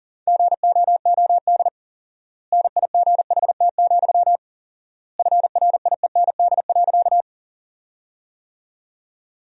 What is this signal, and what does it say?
Morse code, heard as a keyed tone, about 37 wpm, 700 Hz. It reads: GOOD NIGHT, FRIEND.
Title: 4x Word-Spacing - Sentences from Top 300 Words - 40wpm